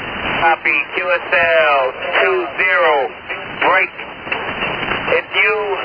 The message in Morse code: none

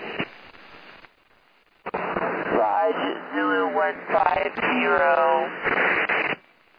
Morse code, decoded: none

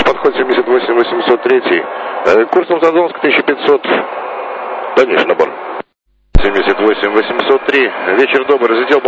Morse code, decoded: none

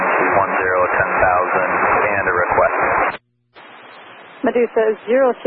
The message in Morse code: none

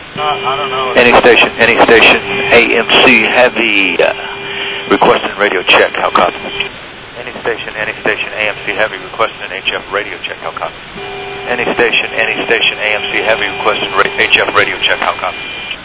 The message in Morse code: none

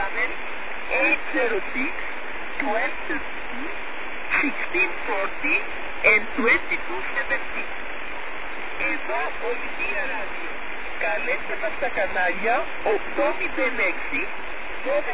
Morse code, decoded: none

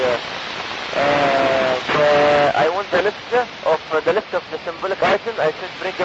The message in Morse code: none

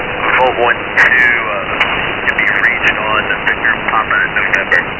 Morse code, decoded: none

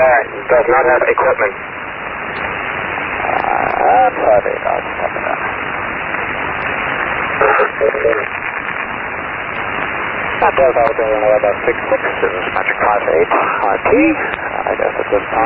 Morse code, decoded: none

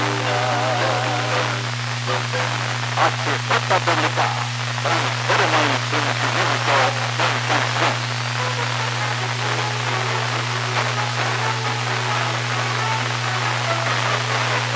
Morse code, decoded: none

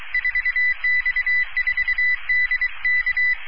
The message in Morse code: 4X4DK

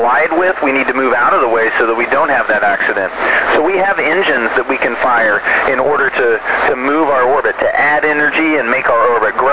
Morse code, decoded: none

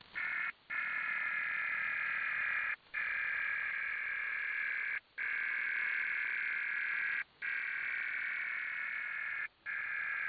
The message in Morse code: none

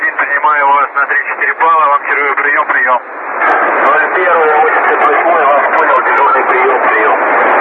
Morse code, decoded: none

